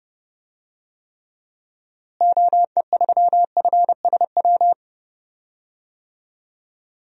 Morse code, heard as a keyed tone, about 30 wpm, 700 Hz. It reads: OE3FSW
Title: DXCC-100 Call Signs - 6 Characters - Repeated 3x - 30wpm